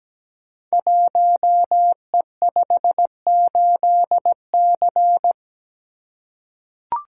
Morse code, heard as a keyed tone, about 17 wpm, 700 Hz. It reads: 1E58C
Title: Code Groups - 5 Letter:Numbers - 17wpm